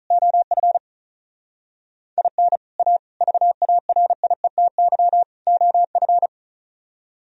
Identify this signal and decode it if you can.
Morse code, heard as a keyed tone, about 45 wpm, 700 Hz. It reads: OF IN A VARIETY OF